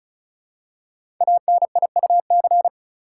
Morse code, heard as a keyed tone, about 36 wpm, 700 Hz. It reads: ANIUC
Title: Code Groups - 5 Letter:Numbers - 35wpm